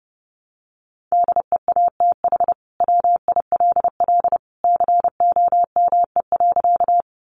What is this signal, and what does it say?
Morse code, heard as a keyed tone, about 28 wpm, 700 Hz. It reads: DEATH WILL COME.
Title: Sentences from Top 1100 Words 30wpm